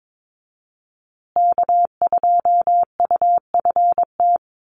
Morse code, heard as a keyed tone, about 22 wpm, 700 Hz. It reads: K2UFT
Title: Call Signs Repeated 3x - 22wpm